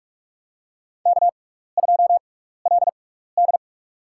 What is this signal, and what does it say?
Morse code, heard as a keyed tone, about 46 wpm, spaced as for 16 wpm, 700 Hz. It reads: K2LD
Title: USA Call Signs - 4 Characters - 6x Character Spacing - 45wpm